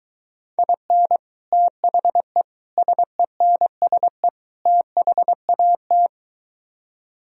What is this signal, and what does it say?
Morse code, heard as a keyed tone, about 23 wpm, 700 Hz. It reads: IN THE SENSE THAT